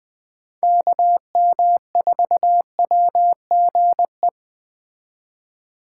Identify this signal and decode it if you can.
Morse code, heard as a keyed tone, about 20 wpm, 700 Hz. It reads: KM4WGE
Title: Speed Racing - USA 2x3 Call Signs 17wpm